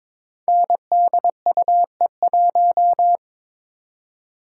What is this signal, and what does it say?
Morse code, heard as a keyed tone, about 22 wpm, 700 Hz. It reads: NDUE1
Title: Code Groups - 5 Letter:Numbers - 22wpm